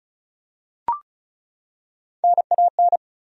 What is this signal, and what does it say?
Morse code, heard as a keyed tone, about 34 wpm, 700 Hz. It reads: NAN
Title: USA Common Names - Max 4 Letters - 35wpm